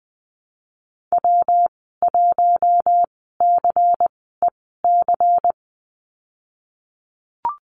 Morse code, heard as a keyed tone, about 20 wpm, spaced as for 15 wpm, 700 Hz. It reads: W1CEC
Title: 179 Lesson 29g - Slash - Call Signs 20wpm